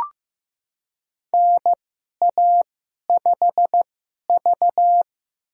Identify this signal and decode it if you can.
Morse code, heard as a keyed tone, about 15 wpm, spaced as for 11 wpm, 700 Hz. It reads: NA5V
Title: USA Call Signs - 2x Character Spacing - 15wpm